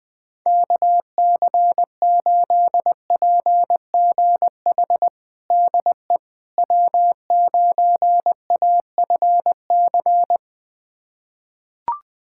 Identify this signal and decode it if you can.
Morse code, heard as a keyed tone, about 20 wpm, 700 Hz. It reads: KC8PGH DE W9AFC
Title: USA Callsign DE Callsign 20wpm